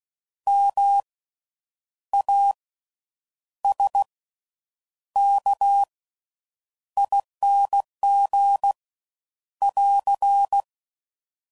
Morse code, heard as a keyed tone, about 16 wpm, spaced as for 8 wpm, 800 Hz. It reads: MASKING <AR>
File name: Slow With Faster Ings